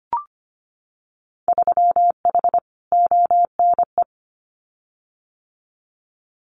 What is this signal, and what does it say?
Morse code, heard as a keyed tone, about 25 wpm, 700 Hz. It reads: 3H ONE
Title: Winter Field Day Exchanges Encore 25wpm